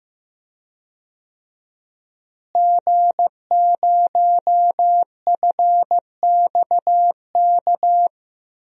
Morse code, heard as a keyed tone, about 15 wpm, 700 Hz. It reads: G0FXK
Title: DXCC-100 Call Signs - 5 Characters - Repeated 2x - 15wpm